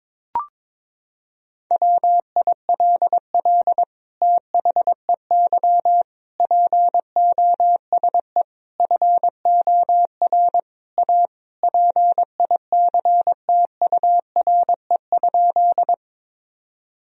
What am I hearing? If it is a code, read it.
Morse code, heard as a keyed tone, about 22 wpm, 700 Hz. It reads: WILL THEY POSE FOR A PICTURE?
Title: Sentences from Top 1000 Words - Review Part 2 - 22wpm